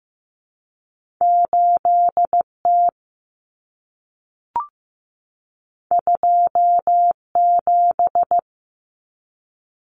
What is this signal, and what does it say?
Morse code, heard as a keyed tone, about 15 wpm, 700 Hz. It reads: 8T 27
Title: Common Cut Numbers - 2-Digit 15wpm